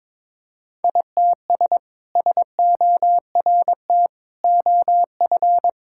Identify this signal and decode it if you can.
Morse code, heard as a keyed tone, about 22 wpm, 700 Hz. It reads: ITS SORT OF
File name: practice / 22wpm / Sentences with 2-Word Phrases - 22wpm